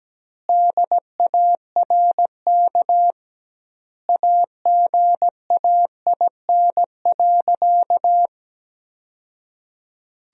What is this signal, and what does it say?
Morse code, heard as a keyed tone, about 17 wpm, 700 Hz. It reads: DARK AGAIN.
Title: 2x Word-Spacing - Sentences from Top 500 Words - 17wpm